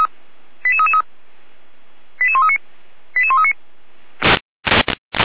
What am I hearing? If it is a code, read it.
Five-tone selective call (ZVEI), heard as quick runs of five tones, tones of about 70 ms. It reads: 88333 88139 88139